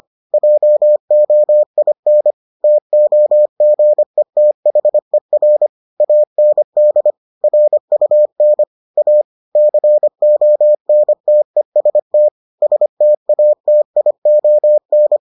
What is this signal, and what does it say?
Morse code, heard as a keyed tone, about 25 wpm, 600 Hz. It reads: JOIN TOGETHER AND RUN A CONTEST STATION